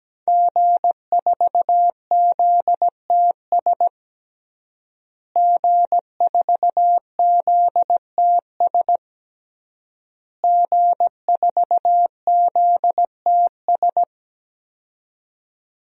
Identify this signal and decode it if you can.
Morse code, heard as a keyed tone, about 17 wpm, 700 Hz. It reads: G4ZTS G4ZTS G4ZTS
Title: DXCC-100 Call Signs - 5 Characters - Repeated 3x - 17wpm